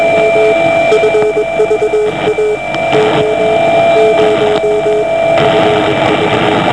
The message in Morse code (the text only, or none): none